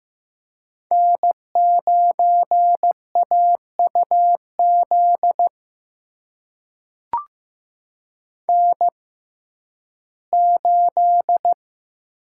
Morse code, heard as a keyed tone, about 15 wpm, spaced as for 11 wpm, 700 Hz. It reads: N9AUZ N8
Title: USA Call Signs - 5 Characters - 6x Character Spacing - 15wpm